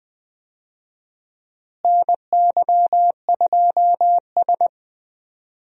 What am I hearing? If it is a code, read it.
Morse code, heard as a keyed tone, about 20 wpm, 700 Hz. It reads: NY2S
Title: USA Call Signs - 4 Characters - Repeated 2x - 20wpm